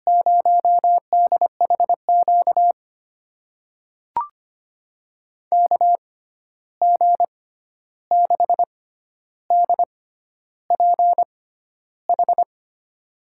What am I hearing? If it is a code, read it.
Morse code, heard as a keyed tone, about 25 wpm, spaced as for 11 wpm, 700 Hz. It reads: 0DHQ KG6DPH
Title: USA Call Signs - 6 Characters - 6x Character Spacing - 25wpm